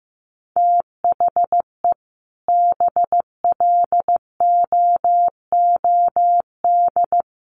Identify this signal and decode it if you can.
Morse code, heard as a keyed tone, about 15 wpm, 700 Hz. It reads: THE BLOOD
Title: Sentences from Top 1200 Words 15wpm